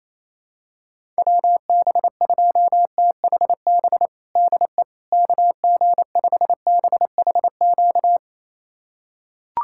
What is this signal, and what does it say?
Morse code, heard as a keyed tone, about 28 wpm, 700 Hz. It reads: WB2THB DE KG5BHQ
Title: USA Callsign DE Callsign 28wpm